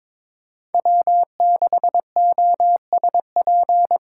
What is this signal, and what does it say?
Morse code, heard as a keyed tone, about 21 wpm, 700 Hz. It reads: W6OSP
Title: K1USN SST QSO Practice - 22wpm